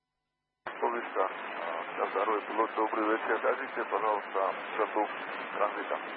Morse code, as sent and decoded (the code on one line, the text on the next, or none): none